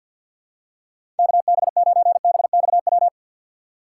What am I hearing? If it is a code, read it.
Morse code, heard as a keyed tone, about 47 wpm, 700 Hz. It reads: KB9BXW
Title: USA Call Signs - 6 Characters - 4x Character Spacing - 50wpm